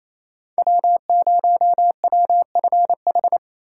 Morse code, heard as a keyed tone, about 36 wpm, 700 Hz. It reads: W0WFH